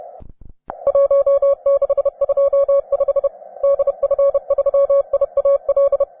Morse code, heard as a keyed tone, about 31 wpm, 560 Hz. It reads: 1625 DF3IAL
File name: wae / cw advanced / k4lta cw